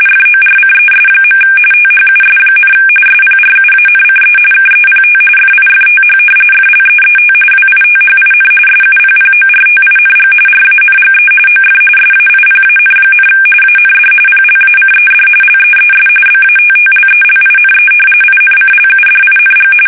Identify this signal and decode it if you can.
RTTY, about 45.45 baud. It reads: FORCES. THUS, SPRUANCE KEPT FIFTH FLEET WITHIN SUPPORTING RANGE OF SAIPAN, AND PLACED LEES BATTLELINE INCLUDING
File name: fsk850 / battleshipnj